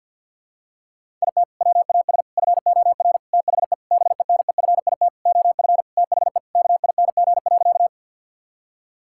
Morse code, heard as a keyed tone, about 52 wpm, 700 Hz. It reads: IT WAS FOR THE BENEFIT OF THE KING.
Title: Sentences from Top 1000 Words - Review Part 6 - 50wpm